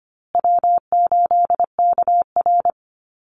Morse code, heard as a keyed tone, about 25 wpm, 700 Hz. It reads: W8KR
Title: Easy Call Signs Repeated 3x - 25wpm